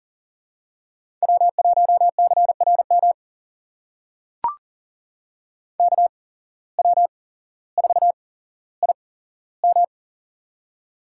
Morse code, heard as a keyed tone, about 40 wpm, spaced as for 17 wpm, 700 Hz. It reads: W1CRM KW4IM
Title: USA Call Signs - 5 Characters - 8x Character Spacing - 40wpm